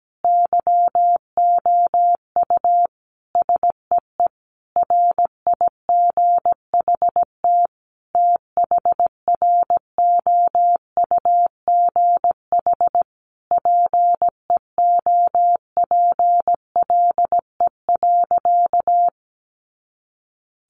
Morse code, heard as a keyed tone, about 17 wpm, 700 Hz. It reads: YOU SEE RIGHT THROUGH PEOPLE.
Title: Sentences from Top 200 Words 17wpm